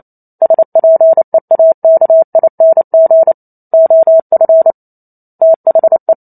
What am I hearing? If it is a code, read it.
Morse code, heard as a keyed tone, about 28 wpm, 650 Hz. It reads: SPEAKING OF THE